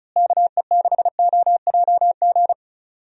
Morse code, heard as a keyed tone, about 36 wpm, 700 Hz. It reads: KE6OJG
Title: USA Call Signs - 6 Characters - Repeated 3x - 35wpm